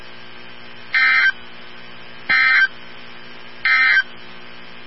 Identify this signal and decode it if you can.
Emergency Alert System (SAME) tones, heard as harsh buzzing bursts. It is heard from 0.8 to 4.1 s.
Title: NOAA Weather Radio 162.550